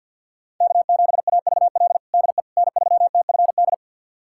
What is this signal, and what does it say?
Morse code, heard as a keyed tone, about 49 wpm, 700 Hz. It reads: K7AVL DE N3TFD
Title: USA Callsign DE Callsign - 5 Characters - 50wpm